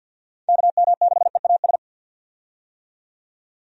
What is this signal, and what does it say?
Morse code, heard as a keyed tone, about 48 wpm, 700 Hz. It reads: KM6EAS
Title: USA Call Signs - 6 Characters - 50wpm